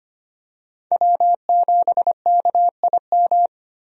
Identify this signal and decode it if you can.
Morse code, heard as a keyed tone, about 25 wpm, 700 Hz. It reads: W7KIM
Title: USA Call Signs - 5 Characters - 25wpm